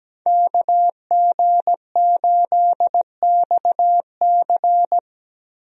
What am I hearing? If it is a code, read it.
Morse code, heard as a keyed tone, about 17 wpm, 700 Hz. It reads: KG8XC
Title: USA Call Signs - 5 Characters - Repeated 3x - 17wpm